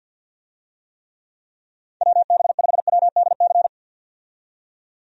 Morse code, heard as a keyed tone, about 50 wpm, 700 Hz. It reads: WB5WDC
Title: USA Call Signs - 6 Characters - 50wpm